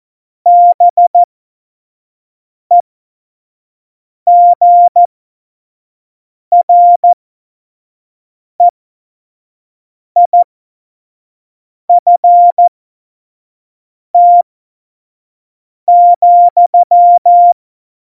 Morse code, heard as a keyed tone, about 14 wpm, spaced as for 5 wpm, 700 Hz. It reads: BEGREIFT,
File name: Morsekurs / Witze / Witz 27. Oktober 2020, 25 F70